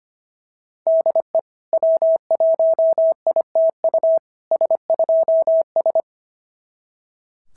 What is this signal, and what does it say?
Morse code, heard as a keyed tone, about 25 wpm, 660 Hz. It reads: DE W1ITU S2S
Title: SOTA QSO Practice 25wpm